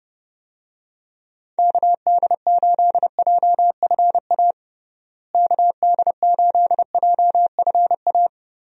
Morse code, heard as a keyed tone, about 32 wpm, 700 Hz. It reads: KD8JFA KD8JFA
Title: USA Call Signs - 6 Characters - Repeated 3x - 30wpm